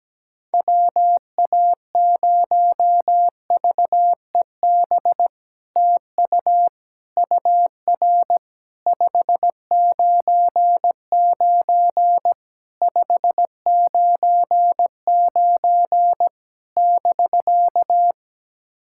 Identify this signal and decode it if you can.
Morse code, heard as a keyed tone, about 17 wpm, 700 Hz. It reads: WA0VEB TU UR 599 599 <BK>
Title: SOTA QSO Practice 17wpm